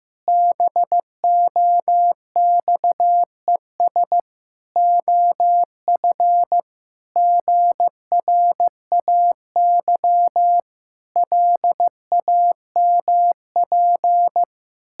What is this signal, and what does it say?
Morse code, heard as a keyed tone, about 15 wpm, 700 Hz. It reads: BOXES OF GRAY LAMP
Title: Pangram Challenge v2 - 15wpm